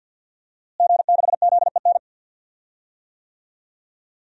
Morse code, heard as a keyed tone, about 51 wpm, 700 Hz. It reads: G6ZEN
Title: DXCC-100 Call Signs - 5 Characters - Repeated 3x - 50wpm